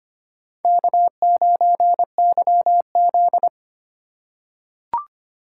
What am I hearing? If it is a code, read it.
Morse code, heard as a keyed tone, about 25 wpm, 700 Hz. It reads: K9YZ
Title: USA Call Signs - 4 Characters - 3x Character Spacing - 25wpm